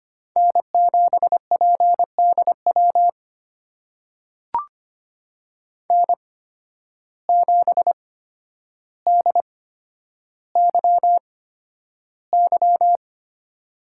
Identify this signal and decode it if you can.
Morse code, heard as a keyed tone, about 25 wpm, spaced as for 11 wpm, 700 Hz. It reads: N7PDW N7DYY
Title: USA Call Signs - 5 Characters - 8x Character Spacing - 25wpm